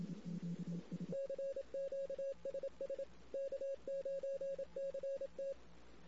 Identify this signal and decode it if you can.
Morse code, heard as a keyed tone, about 27 wpm, 550 Hz. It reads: CQ SS K9CT
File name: CW / 2008-ss-cw-k5tr-op-n5rz -16:00